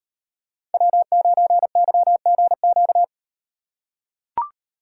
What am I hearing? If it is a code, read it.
Morse code, heard as a keyed tone, about 36 wpm, 700 Hz. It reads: W9YGQ